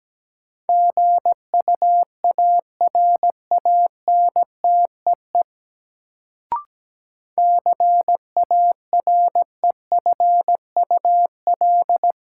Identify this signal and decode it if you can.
Morse code, heard as a keyed tone, about 17 wpm, 700 Hz. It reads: GUARANTEE CAREFUL
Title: Sentences from Top 1600 Words - New Word Drill - 17wpm